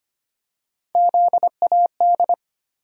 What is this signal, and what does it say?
Morse code, heard as a keyed tone, about 25 wpm, 700 Hz. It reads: ZAD
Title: USA Call Sign Postfixes - 3 Letters 25wpm